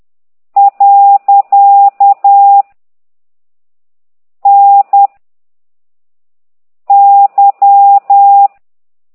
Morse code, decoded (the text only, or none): . N Y